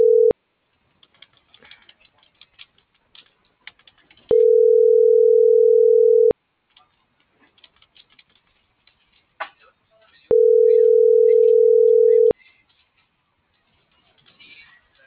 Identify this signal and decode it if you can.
Ringback tone (North America), 440 + 480 Hz.